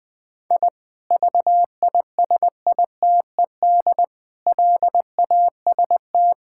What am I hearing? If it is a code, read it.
Morse code, heard as a keyed tone, about 20 wpm, 700 Hz. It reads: I VISITED LAST
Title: Sentences with 2-Word Phrases - 20wpm